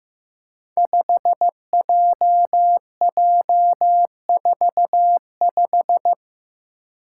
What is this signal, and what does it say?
Morse code, heard as a keyed tone, about 15 wpm, 700 Hz. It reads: 5JJ45
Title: Five Challenging Letter-Number Combinations 15wpm